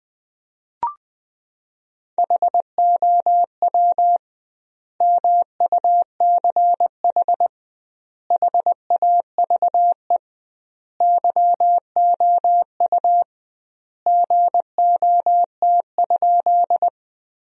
Morse code, heard as a keyed tone, about 20 wpm, 700 Hz. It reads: HOW MUCH HAVE YOU GOT?
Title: 2x Word-Spacing - Sentences from Top 300 Words - 20wpm